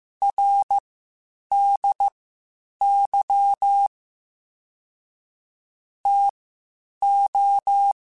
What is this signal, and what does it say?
Morse code, heard as a keyed tone, about 15 wpm, spaced as for 8 wpm, 800 Hz. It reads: RDY TO
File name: (spaced) / QSOs Spaced-mp3s / WA7QOX 1 @13 sp